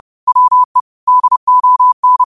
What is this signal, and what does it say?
Morse code, heard as a keyed tone, about 30 wpm, 1000 Hz. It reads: WE DON
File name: qotdfiles / 30wpm